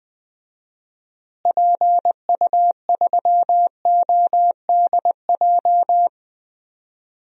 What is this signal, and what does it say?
Morse code, heard as a keyed tone, about 20 wpm, 700 Hz. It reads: PU3ODJ
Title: DXCC-100 Call Signs - 6 Characters - Repeated 3x - 20wpm